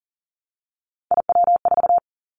Morse code, heard as a keyed tone, about 40 wpm, 700 Hz. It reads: IW4
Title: Callsign Prefixes for Top 50 DXCC - 40wpm